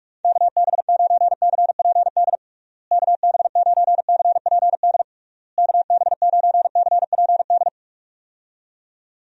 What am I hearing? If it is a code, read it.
Morse code, heard as a keyed tone, about 46 wpm, 700 Hz. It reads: KB9CPD KB9CPD KB9CPD
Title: USA Call Signs - 6 Characters - Repeated 3x - 45wpm